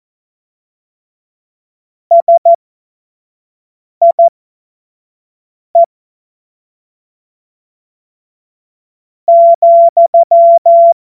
Morse code, heard as a keyed tone, about 14 wpm, spaced as for 4 wpm, 680 Hz. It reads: SIE,
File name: Morsekurs / Witze / Witz 02 April 2024 25 F70